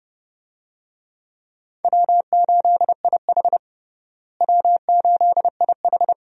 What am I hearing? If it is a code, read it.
Morse code, heard as a keyed tone, about 30 wpm, 700 Hz. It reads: W8IH W8IH
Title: USA Call Signs - 4 Characters - Repeated 2x - 30wpm